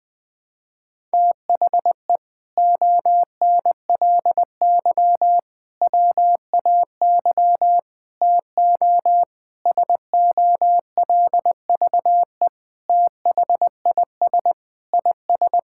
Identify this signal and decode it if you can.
Morse code, heard as a keyed tone, about 21 wpm, 700 Hz. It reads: THE ONLY WAY TO SOLVE THIS IS
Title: Sentences with 4-Word Phrases - Spoken Beginning - 20wpm